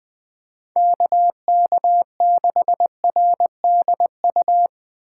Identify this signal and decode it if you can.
Morse code, heard as a keyed tone, about 20 wpm, 700 Hz. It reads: KK6RDU
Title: Speed Racing - USA 2x3 Call Signs 17wpm